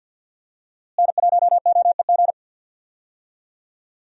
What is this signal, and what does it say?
Morse code, heard as a keyed tone, about 52 wpm, 700 Hz. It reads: N1OEG
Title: USA Call Signs - 5 Characters - Repeated 2x - 50wpm